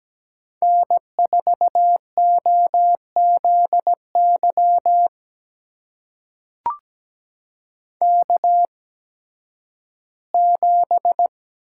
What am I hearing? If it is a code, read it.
Morse code, heard as a keyed tone, about 17 wpm, spaced as for 11 wpm, 700 Hz. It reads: N4OZY K7
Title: USA Call Signs - 5 Characters - 8x Character Spacing - 17wpm